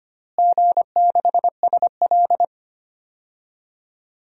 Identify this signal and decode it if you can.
Morse code, heard as a keyed tone, about 25 wpm, 700 Hz. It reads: G6SL
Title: DXCC-100 Call Signs - 4 Characters - Repeated 2x - 25wpm